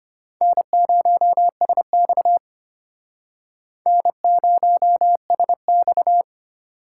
Morse code, heard as a keyed tone, about 33 wpm, 700 Hz. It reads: N0SX N0SX